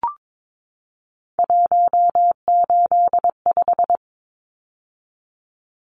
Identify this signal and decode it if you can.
Morse code, heard as a keyed tone, about 22 wpm, 700 Hz. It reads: 185
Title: Numbers - 3-Digit 22wpm